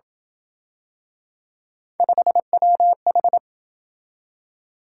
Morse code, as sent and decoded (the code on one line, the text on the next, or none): ..... .-- ....
5WH